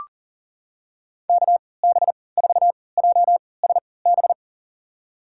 Morse code, heard as a keyed tone, about 40 wpm, spaced as for 23 wpm, 700 Hz. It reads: KB4JSB
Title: USA Call Signs - 6 Characters - 3x Character Spacing - 40wpm